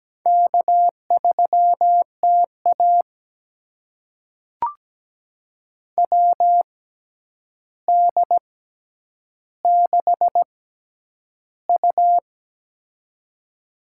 Morse code, heard as a keyed tone, about 17 wpm, spaced as for 9 wpm, 700 Hz. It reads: K3TA WD6U
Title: USA Call Signs - 4 Characters - 6x Character Spacing - 17wpm